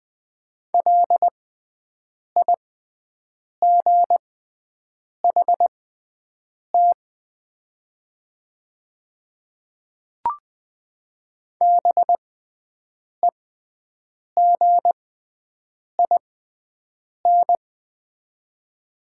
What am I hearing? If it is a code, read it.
Morse code, heard as a keyed tone, about 20 wpm, spaced as for 7 wpm, 700 Hz. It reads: LIGHT BEGIN
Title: 6x Character Spacing - Top 300 Words - 20wpm